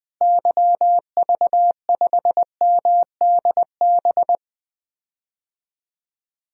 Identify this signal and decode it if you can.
Morse code, heard as a keyed tone, about 20 wpm, 700 Hz. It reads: YV5MDB
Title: DXCC-100 Call Signs - 6 Characters - Repeated 2x - 20wpm